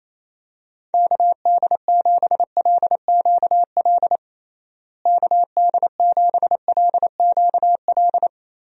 Morse code, heard as a keyed tone, about 28 wpm, 700 Hz. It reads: KD7LQL KD7LQL
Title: USA Call Signs - 6 Characters - Repeated 2x - 28wpm